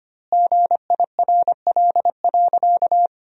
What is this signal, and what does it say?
Morse code, heard as a keyed tone, about 25 wpm, 700 Hz. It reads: GIRL.
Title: 2x Word-Spacing - Sentences from Top 400 Words - 25wpm